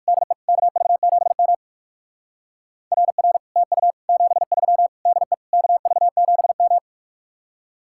Morse code, heard as a keyed tone, about 58 wpm, 700 Hz. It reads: DE KV7M RR TU 73 DE KV7M